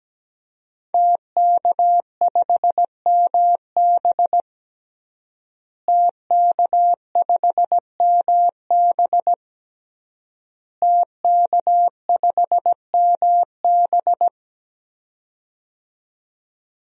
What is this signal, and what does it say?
Morse code, heard as a keyed tone, about 17 wpm, 700 Hz. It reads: TK5MB TK5MB TK5MB
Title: DXCC-100 Call Signs - 5 Characters - Repeated 3x - 17wpm